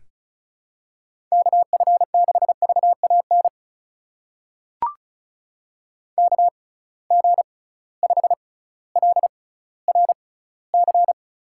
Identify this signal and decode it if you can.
Morse code, heard as a keyed tone, about 36 wpm, spaced as for 18 wpm, 700 Hz. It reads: KF6VAN KG5LRC
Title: USA Call Signs - 6 Characters - 6x Character Spacing - 35wpm